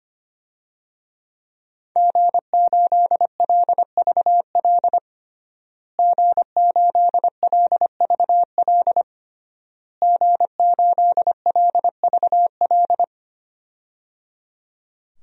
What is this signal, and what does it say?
Morse code, heard as a keyed tone, about 25 wpm, 700 Hz. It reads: G8LVL G8LVL G8LVL
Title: DXCC-100 Call Signs - 5 Characters - Repeated 3x - 25wpm